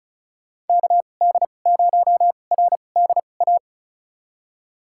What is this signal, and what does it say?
Morse code, heard as a keyed tone, about 36 wpm, spaced as for 25 wpm, 700 Hz. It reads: KD0RDA
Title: USA Call Signs - 6 Characters - 2x Character Spacing - 35wpm